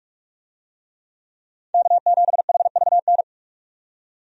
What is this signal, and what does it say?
Morse code, heard as a keyed tone, about 43 wpm, 700 Hz. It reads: K7HVN
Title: USA Call Signs - 5 Characters - 45wpm